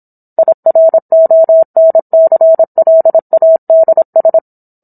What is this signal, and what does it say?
Morse code, heard as a keyed tone, about 27 wpm, 650 Hz. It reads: IRONCLADS